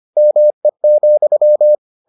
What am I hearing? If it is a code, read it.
Morse code, heard as a keyed tone, about 25 wpm, 600 Hz. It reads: ME,